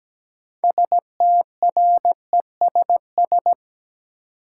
Morse code, heard as a keyed tone, about 17 wpm, 700 Hz. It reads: STRESS
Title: Sentences from Top 1000 Words - Review Part 6 - New Word Drill - 17wpm